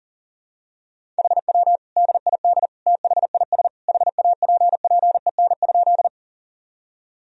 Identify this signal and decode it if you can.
Morse code, heard as a keyed tone, about 40 wpm, 700 Hz. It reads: HW DID THIS HAPPEN?
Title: Sentences from Top 400 Words and Abbreviations 40wpm